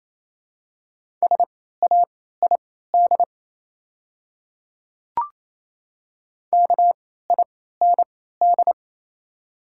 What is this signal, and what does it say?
Morse code, heard as a keyed tone, about 27 wpm, spaced as for 16 wpm, 690 Hz. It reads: SAID KIND
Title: 3x Character Spacing - Top 300 Words - 28wpm